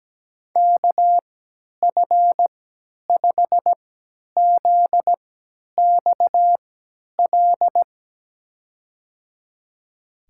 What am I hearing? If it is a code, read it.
Morse code, heard as a keyed tone, about 17 wpm, spaced as for 10 wpm, 700 Hz. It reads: KF5ZXL
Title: USA Call Signs - 6 Characters - 3x Character Spacing - 17wpm